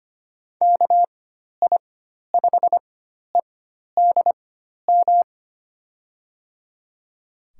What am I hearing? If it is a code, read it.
Morse code, heard as a keyed tone, about 25 wpm, spaced as for 12 wpm, 700 Hz. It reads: KI5EDM
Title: USA Call Signs - 6 Characters - 4x Character Spacing - 25wpm